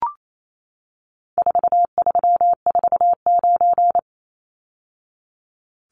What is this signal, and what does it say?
Morse code, heard as a keyed tone, about 28 wpm, 700 Hz. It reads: 4349
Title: Numbers - 4-Digit 28wpm